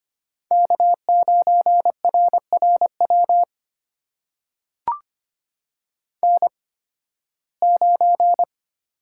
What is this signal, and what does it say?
Morse code, heard as a keyed tone, about 25 wpm, spaced as for 16 wpm, 700 Hz. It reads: K9RRW N9
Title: USA Call Signs - 5 Characters - 8x Character Spacing - 25wpm